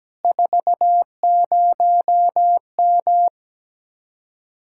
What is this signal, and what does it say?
Morse code, heard as a keyed tone, about 17 wpm, 700 Hz. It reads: 40M
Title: Speed-Racing Common QSO Phrases - 17wpm